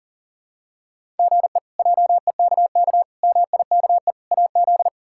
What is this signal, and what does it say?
Morse code, heard as a keyed tone, about 40 wpm, 700 Hz. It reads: GE JEKK MIKE AZ
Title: K1USN SST QSO Practice - 40wpm